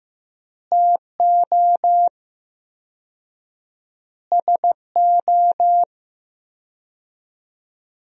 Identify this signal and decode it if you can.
Morse code, heard as a keyed tone, about 15 wpm, 700 Hz. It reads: TO SO
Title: 4x Word-Spacing - Sentences from Top 400 Words - 15wpm